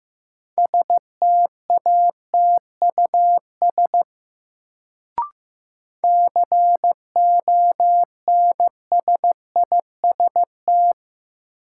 Morse code, heard as a keyed tone, about 15 wpm, 700 Hz. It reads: STATUS CONSIST
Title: Sentences from Top 1400 Words - New Word Drill - 15wpm